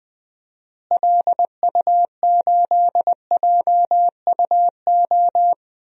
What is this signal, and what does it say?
Morse code, heard as a keyed tone, about 20 wpm, 700 Hz. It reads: LU8JUO
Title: DXCC-100 Call Signs - 6 Characters - Repeated 3x - 20wpm